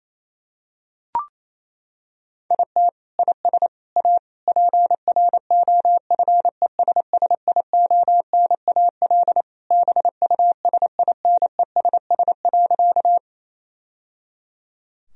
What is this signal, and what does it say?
Morse code, heard as a keyed tone, about 28 wpm, 700 Hz. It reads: IT IS A PROFESSIONAL BUSINESS.
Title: Sentences from Top 1000 Words - Review Part 4 - 28wpm